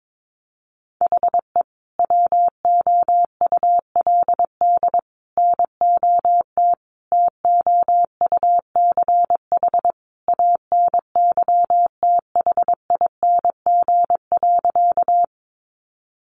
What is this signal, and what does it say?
Morse code, heard as a keyed tone, about 22 wpm, 700 Hz. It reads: HE WOULD NOT TOUCH ANYTHING.
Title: Sentences from Top 1200 Words 22wpm